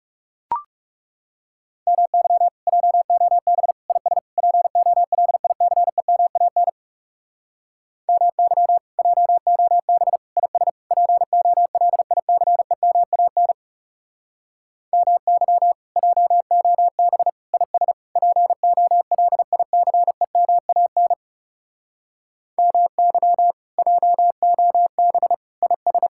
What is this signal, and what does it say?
Morse code, heard as a keyed tone, about 37 wpm, 700 Hz. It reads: MY JOB IS POLICEMAN MY JOB IS POLICEMAN MY JOB IS POLICEMAN MY JOB IS
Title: Speed-Racing Common QSO Phrases Encore - 30wpm